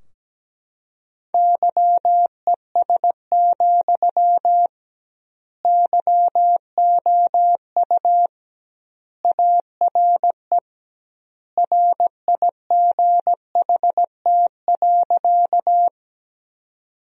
Morse code, heard as a keyed tone, about 17 wpm, 700 Hz. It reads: YES, YOU ARE RIGHT.
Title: 2x Word-Spacing - Sentences from Top 500 Words - 17wpm